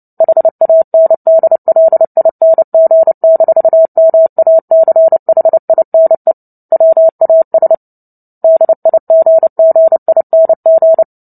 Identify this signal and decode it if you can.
Morse code, heard as a keyed tone, about 30 wpm, 650 Hz. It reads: HANDLING-MACHINE WAS DIGGING
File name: Book Two / 03-Chapter 3 - 29 WPM